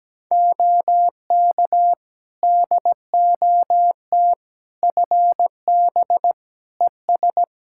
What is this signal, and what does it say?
Morse code, heard as a keyed tone, about 17 wpm, 700 Hz. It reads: OK DOT FB ES
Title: Practice QSO - Exchange 2 - Info - 17wpm